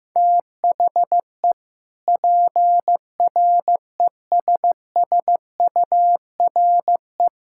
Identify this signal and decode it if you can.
Morse code, heard as a keyed tone, about 15 wpm, 700 Hz. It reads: THE PRESSURE
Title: Sentences from Top 1000 Words - Review Part 6 - 15wpm